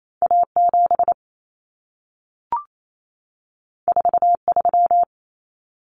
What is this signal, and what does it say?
Morse code, heard as a keyed tone, about 28 wpm, 700 Hz. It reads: A7 43
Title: Common Cut Numbers - 2-Digit 28wpm